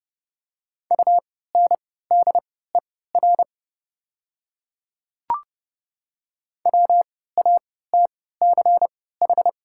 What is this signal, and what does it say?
Morse code, heard as a keyed tone, about 30 wpm, spaced as for 17 wpm, 700 Hz. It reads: UNDER WATCH
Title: 3x Character Spacing - Top 300 Words - 30wpm